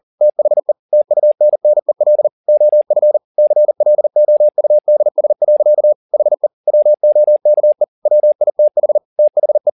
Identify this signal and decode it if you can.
Morse code, heard as a keyed tone, about 40 wpm, 600 Hz. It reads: THE TUNNEL OF CLOUDS. HE WOKE WITH THE
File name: the old man and the sea 2 40wpm